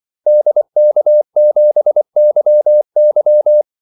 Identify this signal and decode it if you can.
Morse code, heard as a keyed tone, about 24 wpm, 600 Hz. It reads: DK7YY
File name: Callsigns-24wpm